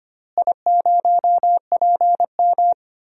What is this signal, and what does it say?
Morse code, heard as a keyed tone, about 25 wpm, 700 Hz. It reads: I0PM
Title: DXCC-100 Call Signs - 4 Characters - Repeated 3x - 25wpm